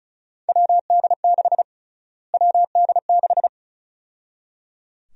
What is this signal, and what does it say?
Morse code, heard as a keyed tone, about 34 wpm, 700 Hz. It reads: WD6 WD6
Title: USA Call Sign Prefixes - Repeated 2x - 35wpm